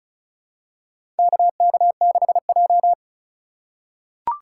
Morse code, heard as a keyed tone, about 36 wpm, 700 Hz. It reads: KK6J
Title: USA Call Signs - 4 Characters - 3x Character Spacing - 35wpm